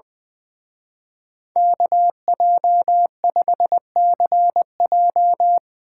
Morse code, heard as a keyed tone, about 21 wpm, 700 Hz. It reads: KJ5CJ
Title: Speed Racing - USA Call Signs 20wpm